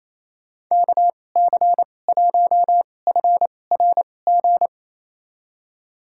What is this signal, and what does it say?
Morse code, heard as a keyed tone, about 28 wpm, spaced as for 20 wpm, 700 Hz. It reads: KC1FRG
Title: USA Call Signs - 6 Characters - 2x Character Spacing - 28wpm